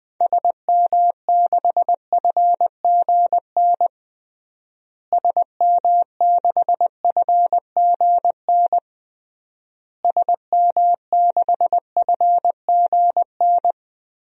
Morse code, heard as a keyed tone, about 20 wpm, 700 Hz. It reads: SM6FGN SM6FGN SM6FGN
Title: DXCC-100 Call Signs - 6 Characters - Repeated 3x - 20wpm